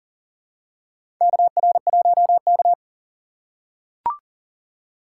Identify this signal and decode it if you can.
Morse code, heard as a keyed tone, about 40 wpm, 700 Hz. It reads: KR1K
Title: USA Call Signs - 4 Characters - 3x Character Spacing - 40wpm